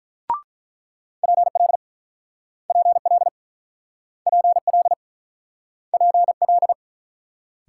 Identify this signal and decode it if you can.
Morse code, heard as a keyed tone, about 50 wpm, 700 Hz. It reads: PL PL PL PL